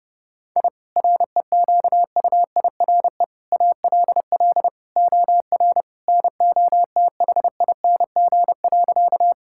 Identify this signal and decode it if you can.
Morse code, heard as a keyed tone, about 30 wpm, 700 Hz. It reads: I REQUIRE ALL OR NOTHING.
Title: Sentences from Top 1000 Words - Review Part 6 - 30wpm